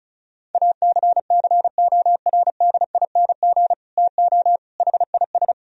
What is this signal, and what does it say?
Morse code, heard as a keyed tone, about 34 wpm, 700 Hz. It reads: ACCORDING TO HIS
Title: Sentences from Top 1000 Words - Review Part 4 - 35wpm